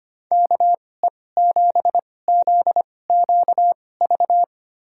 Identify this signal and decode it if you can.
Morse code, heard as a keyed tone, about 25 wpm, spaced as for 18 wpm, 700 Hz. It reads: KE7ZQV
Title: USA Call Signs - 2x Character Spacing - 25wpm